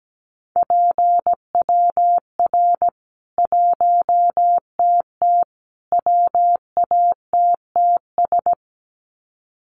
Morse code, heard as a keyed tone, about 17 wpm, 700 Hz. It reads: PWR 1TT WATTS
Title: Common QSO Phrases 17wpm